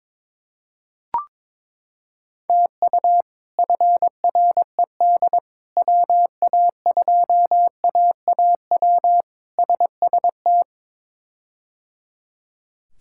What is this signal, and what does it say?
Morse code, heard as a keyed tone, about 22 wpm, 700 Hz. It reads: TU FRED WA2AAW SST
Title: K1USN SST QSO Practice - 22wpm